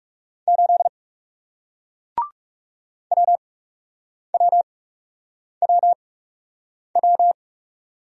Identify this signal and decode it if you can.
Morse code, heard as a keyed tone, about 41 wpm, spaced as for 8 wpm, 700 Hz. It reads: 8 WWWW